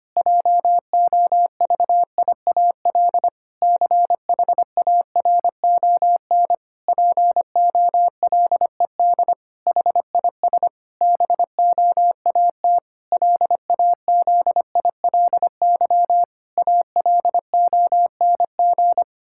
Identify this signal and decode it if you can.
Morse code, heard as a keyed tone, about 25 wpm, 700 Hz. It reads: JOVIAL CHARON POLED HIS BOAT LAZILY ALONG
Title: Pangram Challenge v2 - 25wpm